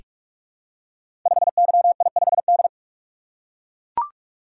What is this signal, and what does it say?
Morse code, heard as a keyed tone, about 44 wpm, 700 Hz. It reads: 5YI5D
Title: Code Groups - 5 Letter:Numbers - 45wpm